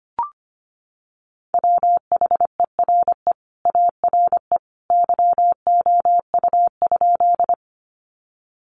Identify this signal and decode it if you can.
Morse code, heard as a keyed tone, about 25 wpm, 700 Hz. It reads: WHERE ARE YOU?
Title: Sentences from Top 200 Words 25wpm